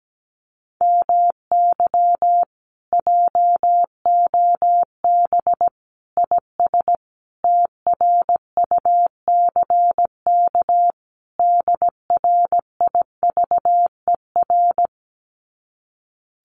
Morse code, heard as a keyed tone, about 17 wpm, 700 Hz. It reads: MY JOB IS TRUCK DRIVER
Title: Common QSO Phrases Encore 17wpm